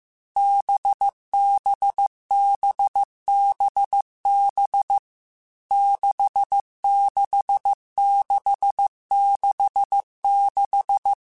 Morse code, heard as a keyed tone, about 15 wpm, 800 Hz. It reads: BBBBB 66666
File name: Short Runs / Short Runs-mp3s / NDB NDB6 @13